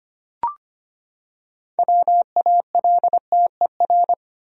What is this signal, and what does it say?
Morse code, heard as a keyed tone, about 25 wpm, 700 Hz. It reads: WALTER
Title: USA Common Names - 25wpm